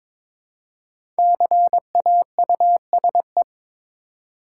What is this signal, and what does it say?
Morse code, heard as a keyed tone, about 22 wpm, 690 Hz. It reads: CAUSE
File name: 5 Letter Words - First 3 Letters Given - 22wpm